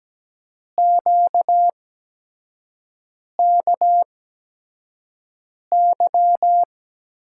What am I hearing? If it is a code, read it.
Morse code, heard as a keyed tone, about 17 wpm, spaced as for 5 wpm, 700 Hz. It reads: QKY
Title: USA Call Signs - 6 Characters - 8x Character Spacing - 17wpm